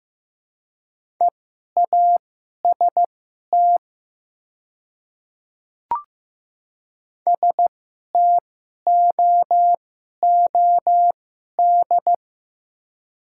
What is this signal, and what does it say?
Morse code, heard as a keyed tone, about 15 wpm, spaced as for 11 wpm, 700 Hz. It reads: EAST STOOD
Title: 2x Character Spacing - Top 500 Words - 15wpm